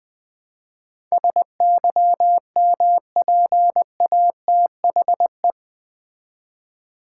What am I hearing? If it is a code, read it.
Morse code, heard as a keyed tone, about 20 wpm, 700 Hz. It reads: SYMPATHE
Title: Build A Word - 4 Letters At A Time - 11-Letter Words - 20wpm